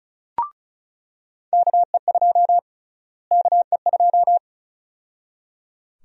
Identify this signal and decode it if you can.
Morse code, heard as a keyed tone, about 36 wpm, 700 Hz. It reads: KE2 KE2
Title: USA Call Sign Prefixes - Repeated 2x - 35wpm